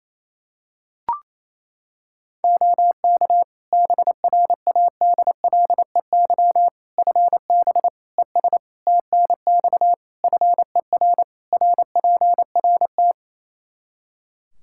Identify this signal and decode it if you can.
Morse code, heard as a keyed tone, about 28 wpm, 700 Hz. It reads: OK BRADLEY FB ES TNX FER RPRT
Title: Practice QSO - Exchange 2 - Info - 28wpm